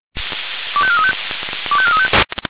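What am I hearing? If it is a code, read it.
Five-tone selective call (ZVEI), heard as quick runs of five tones, tones of about 70 ms. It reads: 25435 25435
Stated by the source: ZVEI five-tone selective call received on SDR.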